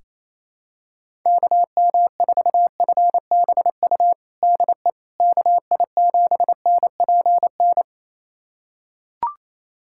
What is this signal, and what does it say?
Morse code, heard as a keyed tone, about 28 wpm, 700 Hz. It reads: KM4FBU DE KI7NPN
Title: USA Callsign DE Callsign 28wpm